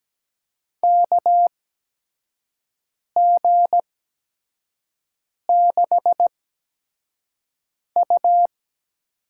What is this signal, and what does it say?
Morse code, heard as a keyed tone, about 17 wpm, spaced as for 5 wpm, 700 Hz. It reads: KG6U
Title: USA Call Signs - 5 Characters - 8x Character Spacing - 17wpm